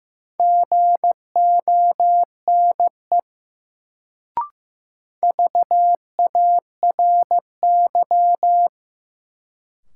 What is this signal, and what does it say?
Morse code, heard as a keyed tone, about 15 wpm, 700 Hz. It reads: GONE VARY
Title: Sentences from Top 1200 Words - New Word Drill - 15wpm